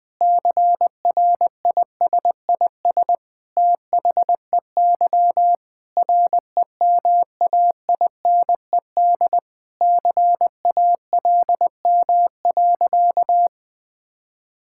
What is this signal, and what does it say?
Morse code, heard as a keyed tone, about 21 wpm, 700 Hz. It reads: CRISIS THEY REMAINED CALM.
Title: Sentences with 5-Word Phrases - 20wpm